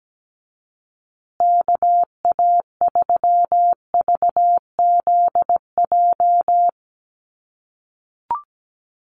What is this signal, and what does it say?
Morse code, heard as a keyed tone, about 17 wpm, 700 Hz. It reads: KA3VZJ
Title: USA Call Signs Repeated 3x 17wpm